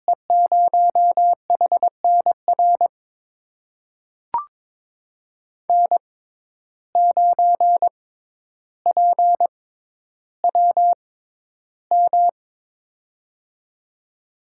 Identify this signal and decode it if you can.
Morse code, heard as a keyed tone, about 22 wpm, spaced as for 11 wpm, 700 Hz. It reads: E0HNR N9PWM
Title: USA Call Signs - 6x Character Spacing - 22wpm